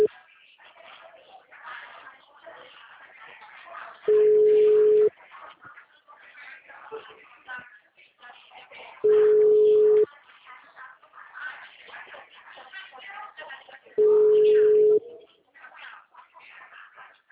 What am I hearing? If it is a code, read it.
Ringback tone (Europe, 425 Hz), with speech.